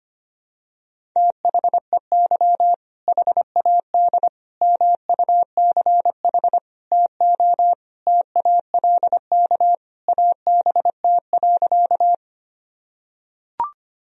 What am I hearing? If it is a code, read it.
Morse code, heard as a keyed tone, about 25 wpm, 700 Hz. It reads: THEY HAD MUCH TO TALK ABT.
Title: Sentences from Top 400 Words and Abbreviations 25wpm